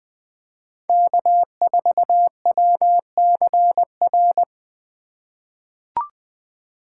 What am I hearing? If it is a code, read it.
Morse code, heard as a keyed tone, about 20 wpm, 700 Hz. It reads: K4WCR
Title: USA Call Signs - 5 Characters - 3x Character Spacing - 20wpm